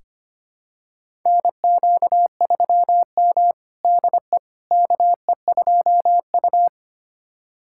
Morse code, heard as a keyed tone, about 25 wpm, 700 Hz. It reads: NQ3M DE KE2U
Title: USA Callsign DE Callsign - 4 Characters - 25wpm